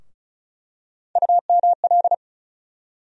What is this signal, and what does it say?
Morse code, heard as a keyed tone, about 36 wpm, 700 Hz. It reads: UML
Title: USA Call Sign Postfixes - 3 Letters 35wpm